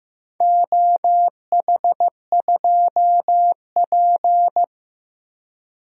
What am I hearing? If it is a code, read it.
Morse code, heard as a keyed tone, about 15 wpm, 700 Hz. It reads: OH2P
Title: DXCC-100 Call Signs - 4 Characters - Repeated 3x - 15wpm